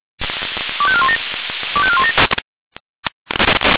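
Five-tone selective call (ZVEI), heard as quick runs of five tones, tones of about 80 ms. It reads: 25417 25417
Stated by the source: Five-tone ZVEI selective call received on SDR.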